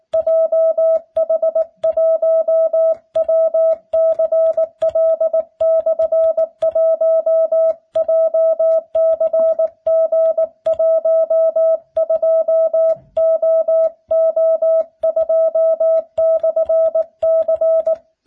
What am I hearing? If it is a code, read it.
Morse code, heard as a keyed tone, about 19 wpm, 650 Hz. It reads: JH1WCL/1 JCG12OO2/C